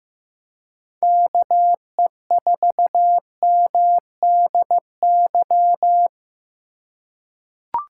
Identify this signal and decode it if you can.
Morse code, heard as a keyed tone, about 15 wpm, 700 Hz. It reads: KE4MDY
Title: USA Call Signs - 6 Characters - 2x Character Spacing - 15wpm